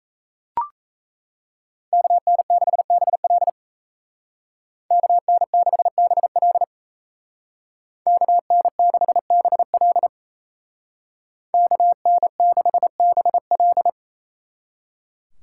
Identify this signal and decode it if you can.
Morse code, heard as a keyed tone, about 37 wpm, 700 Hz. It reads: KN6BL KN6BL KN6BL KN6BL